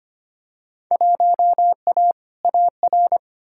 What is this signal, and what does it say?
Morse code, heard as a keyed tone, about 25 wpm, 700 Hz. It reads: 1A AR
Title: Field Day Exchanges Encore - 25wpm